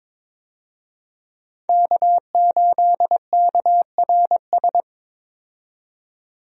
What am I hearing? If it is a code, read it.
Morse code, heard as a keyed tone, about 22 wpm, 700 Hz. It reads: K8KRS
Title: USA Call Signs - 5 Characters - 22wpm